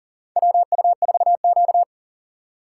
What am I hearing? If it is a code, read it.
Morse code, heard as a keyed tone, about 40 wpm, 700 Hz. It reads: WU4Q
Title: USA Call Signs - 4 Characters - 2x Character Spacing - 40wpm